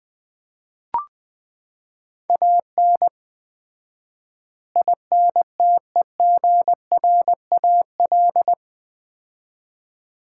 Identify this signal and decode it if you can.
Morse code, heard as a keyed tone, about 20 wpm, 700 Hz. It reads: AN INTEGRAL
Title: 4x Word-Spacing - Sets of 5 Words - 20wpm